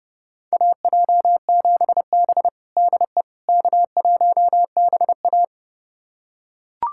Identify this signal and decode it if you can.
Morse code, heard as a keyed tone, about 30 wpm, 700 Hz. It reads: AJ7B DE K1BA
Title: USA Callsign DE Callsign - 4 Characters - 30wpm